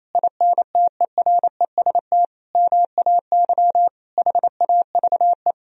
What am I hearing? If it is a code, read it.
Morse code, heard as a keyed tone, about 28 wpm, 700 Hz. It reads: INTEREST MAY HAVE
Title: Sentences from Top 1000 Words - Review Part 2 - 28wpm